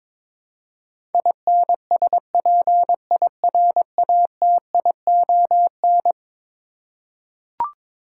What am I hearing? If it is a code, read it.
Morse code, heard as a keyed tone, about 21 wpm, 700 Hz. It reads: INSPIRATION
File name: practice / 22wpm / Eleven Letter Words - 22wpm